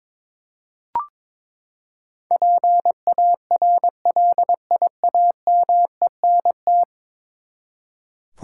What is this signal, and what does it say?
Morse code, heard as a keyed tone, about 21 wpm, 700 Hz. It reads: PARLIAMENT
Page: morse-code-ninja-practice-22wpm